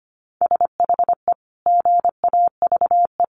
Morse code, heard as a keyed tone, about 25 wpm, 700 Hz. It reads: SHE GAVE
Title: Sentences from Top 1400 Words 25wpm